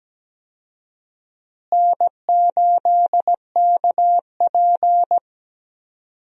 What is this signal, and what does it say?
Morse code, heard as a keyed tone, about 17 wpm, 700 Hz. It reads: N8KP
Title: USA Call Signs - 4 Characters - Repeated 2x - 17wpm